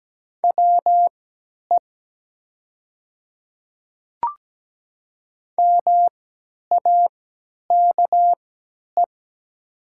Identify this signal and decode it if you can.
Morse code, heard as a keyed tone, about 17 wpm, spaced as for 10 wpm, 700 Hz. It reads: WE MAKE
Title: 3x Character Spacing - Top 100 Words - 17wpm